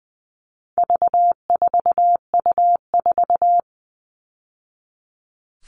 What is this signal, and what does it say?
Morse code, heard as a keyed tone, about 20 wpm, 700 Hz. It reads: V4U4
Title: All Cut Numbers - 4-Digit 20wpm